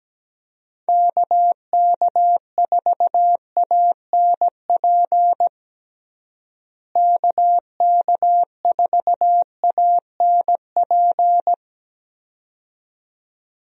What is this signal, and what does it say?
Morse code, heard as a keyed tone, about 17 wpm, 700 Hz. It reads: KK4ANP KK4ANP
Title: USA Call Signs - 6 Characters - Repeated 2x - 17wpm